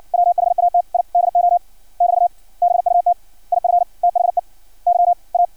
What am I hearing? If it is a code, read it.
Morse code, heard as a keyed tone, about 56 wpm, spaced as for 46 wpm, 700 Hz. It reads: WHAT I DO <BT> BUT IF THEY A
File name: archived / CRSnet